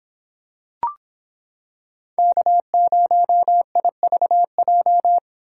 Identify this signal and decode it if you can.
Morse code, heard as a keyed tone, about 26 wpm, 700 Hz. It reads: K0IVJ